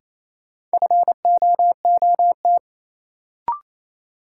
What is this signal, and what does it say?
Morse code, heard as a keyed tone, about 28 wpm, 700 Hz. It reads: FOOT
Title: Sentences from Top 500 Words - New Word Drill - 28wpm